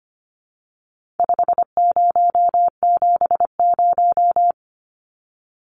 Morse code, heard as a keyed tone, about 25 wpm, 700 Hz. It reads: 5070
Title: Numbers - 4-Digit 25wpm